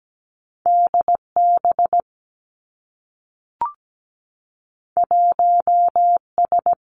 Morse code, heard as a keyed tone, about 17 wpm, 700 Hz. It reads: DB 1S
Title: Two Challenging Letter-Number Combinations 17wpm